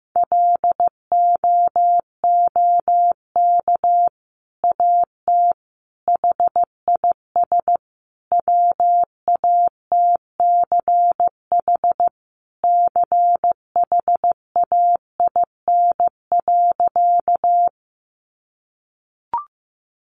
Morse code, heard as a keyed tone, about 15 wpm, 700 Hz. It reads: LOOK AT HIS WATCH CHAIN.
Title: Sentences from Top 1500 Words 15wpm